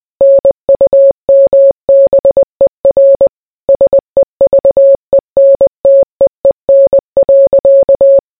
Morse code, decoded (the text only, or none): NUMBER SEVENTEEN.